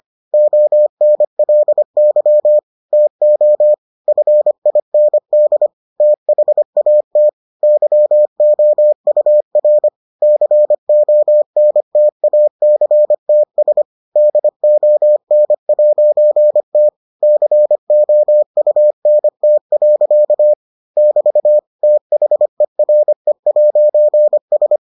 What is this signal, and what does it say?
Morse code, heard as a keyed tone, about 25 wpm, 600 Hz. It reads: ONLY TO FIND THAT YOUR CONTACTS DON'T COUNT. <BT> THERE'S